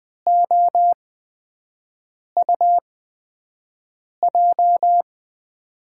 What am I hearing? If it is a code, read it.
Morse code, heard as a keyed tone, about 20 wpm, spaced as for 5 wpm, 700 Hz. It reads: OUJ
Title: USA Call Signs - 6 Characters - 8x Character Spacing - 20wpm